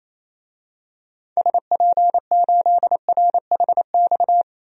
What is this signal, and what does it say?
Morse code, heard as a keyed tone, about 28 wpm, 700 Hz. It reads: SP8RHX
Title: DXCC-100 Call Signs - 6 Characters - 28wpm